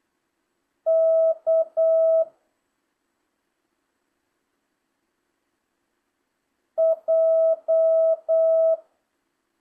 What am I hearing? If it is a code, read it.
Morse code, heard as a keyed tone, about 8 wpm, 650 Hz. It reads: K J